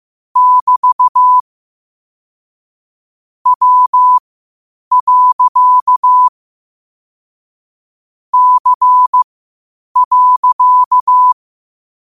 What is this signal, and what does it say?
Morse code, heard as a keyed tone, about 15 wpm, spaced as for 8 wpm, 1000 Hz. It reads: <BT> W. C.
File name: qotdfiles / 8wpm